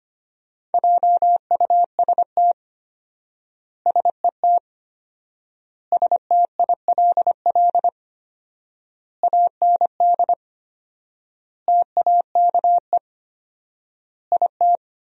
Morse code, heard as a keyed tone, about 25 wpm, 700 Hz. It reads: JUST SET STILL AND TAKE IT
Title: 4x Word-Spacing - Sentences from Top 300 Words - 25wpm